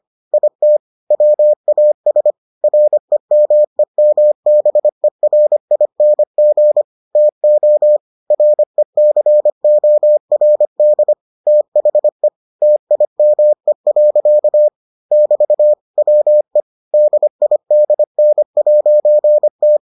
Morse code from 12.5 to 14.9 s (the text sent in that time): TIME.